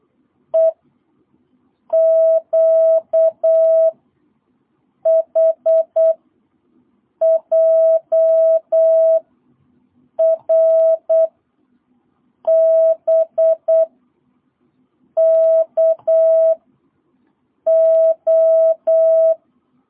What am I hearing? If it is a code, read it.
Morse code, heard as a keyed tone, about 8 wpm, 650 Hz. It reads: E Q H J R B K O
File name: jh8cbh / 2022cw